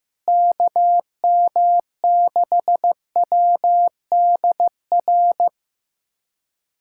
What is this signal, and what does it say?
Morse code, heard as a keyed tone, about 16 wpm, 700 Hz. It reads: KM6WDR
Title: USA Callsign DE Callsign - 6 Characters - 15wpm